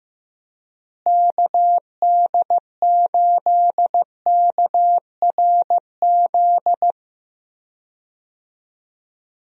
Morse code, heard as a keyed tone, about 15 wpm, 700 Hz. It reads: KD8KRZ
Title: USA Call Signs - 6 Characters - Repeated 2x - 15wpm